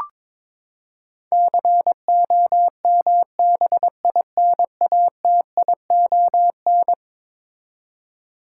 Morse code, heard as a keyed tone, about 22 wpm, 700 Hz. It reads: COMBINATION
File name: practice / 22wpm / Eleven Letter Words - 22wpm